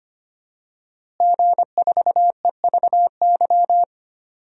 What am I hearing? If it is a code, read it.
Morse code, heard as a keyed tone, about 25 wpm, 700 Hz. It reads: G4EVY
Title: DXCC-100 Call Signs - 5 Characters - Repeated 3x - 25wpm